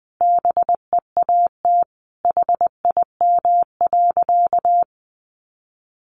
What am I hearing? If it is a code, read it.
Morse code, heard as a keyed tone, about 20 wpm, 700 Hz. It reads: BEAT HIM.
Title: Sentences from Top 1300 Words 20wpm